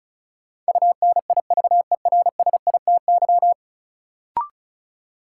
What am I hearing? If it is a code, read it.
Morse code, heard as a keyed tone, about 34 wpm, 700 Hz. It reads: UNIVERSITY
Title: Sentences from Top 1300 Words - New Word Drill Alt - 35wpm